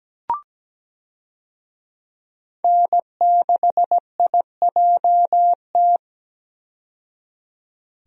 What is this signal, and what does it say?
Morse code, heard as a keyed tone, about 17 wpm, 700 Hz. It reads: N6IJT
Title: USA Call Signs - 5 Characters - 17wpm